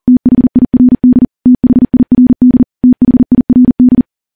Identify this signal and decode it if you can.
Morse code, heard as a keyed tone, about 43 wpm, 250 Hz. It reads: THIRD THIRD THIRD
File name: morse / morse-training-40WPM